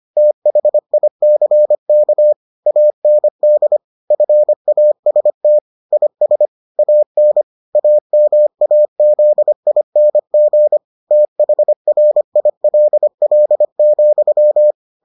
Morse code, heard as a keyed tone, about 25 wpm, 600 Hz. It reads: THICK AND FAST IS AN AMAZING THRILL,